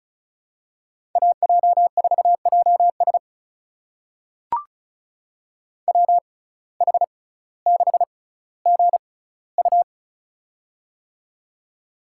Morse code, heard as a keyed tone, about 36 wpm, spaced as for 18 wpm, 700 Hz. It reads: AJ4JS WH6GU
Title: USA Call Signs - 5 Characters - 6x Character Spacing - 35wpm